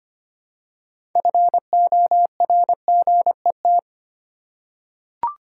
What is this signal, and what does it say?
Morse code, heard as a keyed tone, about 25 wpm, 700 Hz. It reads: FORGET
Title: Six Letter Words - 25wpm